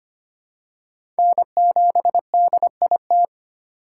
Morse code, heard as a keyed tone, about 25 wpm, 700 Hz. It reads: N7DIT
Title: USA Call Signs - 5 Characters - 3x Character Spacing - 25wpm